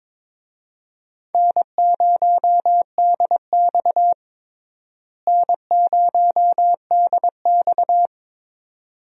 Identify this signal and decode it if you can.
Morse code, heard as a keyed tone, about 22 wpm, 700 Hz. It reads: N0DX N0DX
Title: USA Call Signs - 4 Characters - Repeated 2x - 22wpm